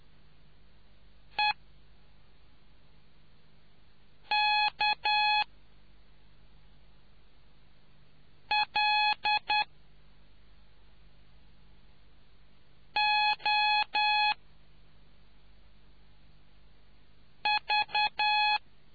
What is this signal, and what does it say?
Morse code, heard as a keyed tone, about 10 wpm, spaced as for 3 wpm, 840 Hz. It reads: EKLOV